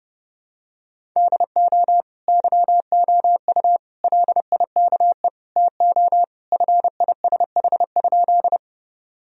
Morse code, heard as a keyed tone, about 32 wpm, 700 Hz. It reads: DO YOU LIKE TO FISH?
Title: Sentences from Top 1000 Words - Review Part 1 - 30wpm